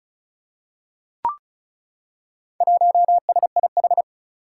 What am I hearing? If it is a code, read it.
Morse code, heard as a keyed tone, about 34 wpm, 700 Hz. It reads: 1SIH
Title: Four Challenging Letter-Number Combinations 35wpm